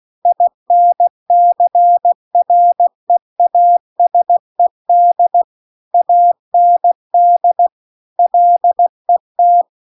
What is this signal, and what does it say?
Morse code, heard as a keyed tone, about 16 wpm, 700 Hz. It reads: INCREASED AND LET
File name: CW-The-Big-Twoheaded-River-2-16WPM